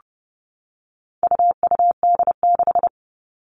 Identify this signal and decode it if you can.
Morse code, heard as a keyed tone, about 30 wpm, 700 Hz. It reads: UUD6
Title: All Cut Numbers - 4-Digit 30wpm